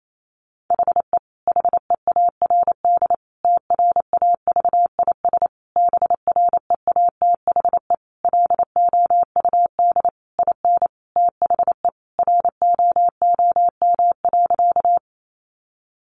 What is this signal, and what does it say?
Morse code, heard as a keyed tone, about 28 wpm, 700 Hz. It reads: HE HEARD TRAVIS BREATHE LOUD IN THE ROOM.
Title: A Sound Of Thunder 28wpm